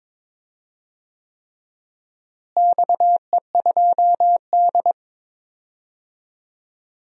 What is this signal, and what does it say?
Morse code, heard as a keyed tone, about 22 wpm, 700 Hz. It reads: XE2D
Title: DXCC-100 Call Signs - 4 Characters - Repeated 3x - 22wpm